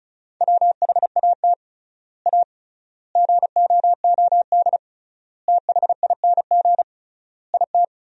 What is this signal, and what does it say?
Morse code, heard as a keyed tone, about 34 wpm, 700 Hz. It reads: WHAT A GOOD THING IT
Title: 3x Word-Spacing - Sentences from Top 200 Words - 35wpm